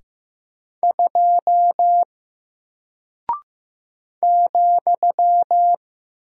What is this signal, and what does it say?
Morse code, heard as a keyed tone, about 15 wpm, 700 Hz. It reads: 2 ,
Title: Single Character (Letter:Number:Punctuation:Prosign) Alt 15wpm